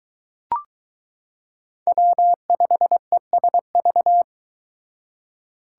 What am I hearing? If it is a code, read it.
Morse code, heard as a keyed tone, about 22 wpm, 700 Hz. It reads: W5ESV